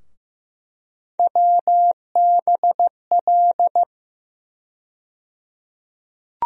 Morse code, heard as a keyed tone, about 15 wpm, 700 Hz. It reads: WBL
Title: USA Call Sign Postfixes - 3 Letters 15wpm